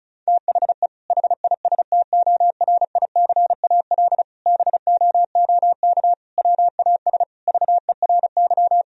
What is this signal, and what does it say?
Morse code, heard as a keyed tone, about 34 wpm, 700 Hz. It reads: THE HISTORICAL BOOK WAS VERY
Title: Sentences from Top 1000 Words - Review Part 4 - 35wpm